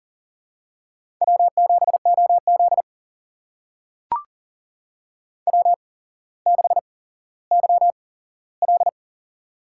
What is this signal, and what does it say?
Morse code, heard as a keyed tone, about 40 wpm, spaced as for 17 wpm, 700 Hz. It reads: W7OZ W6YL
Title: USA Call Signs - 4 Characters - 8x Character Spacing - 40wpm